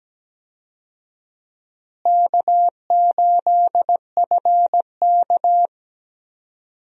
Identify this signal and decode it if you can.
Morse code, heard as a keyed tone, about 17 wpm, 700 Hz. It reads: K8FK
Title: USA Call Signs - 4 Characters - Repeated 2x - 17wpm